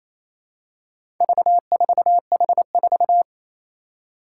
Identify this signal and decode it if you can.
Morse code, heard as a keyed tone, about 28 wpm, 700 Hz. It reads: V4H4
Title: Four Challenging Letter-Number Combinations 28wpm